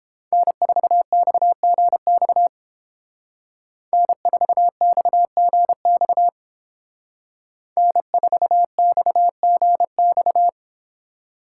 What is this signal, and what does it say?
Morse code, heard as a keyed tone, about 30 wpm, 700 Hz. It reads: N4XGX N4XGX N4XGX